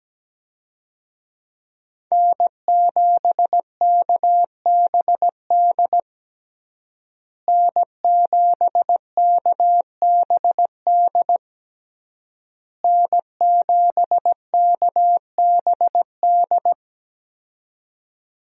Morse code, heard as a keyed tone, about 17 wpm, 700 Hz. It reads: N7KBD N7KBD N7KBD
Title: USA Call Signs - 5 Characters - Repeated 3x - 17wpm